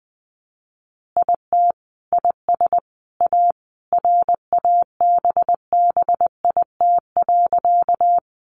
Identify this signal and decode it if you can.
Morse code, heard as a keyed tone, about 20 wpm, 700 Hz. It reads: IT IS A RABBIT.
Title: Easy Phrases 20wpm